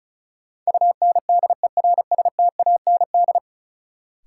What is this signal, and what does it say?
Morse code, heard as a keyed tone, about 36 wpm, 700 Hz. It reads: UNDERSTAND
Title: Sets of 4 Words - New Word Drill - 35wpm